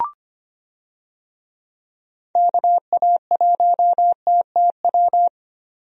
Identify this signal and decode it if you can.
Morse code, heard as a keyed tone, about 25 wpm, 700 Hz. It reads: KA1TTW
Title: USA Call Signs - 6 Characters - 25wpm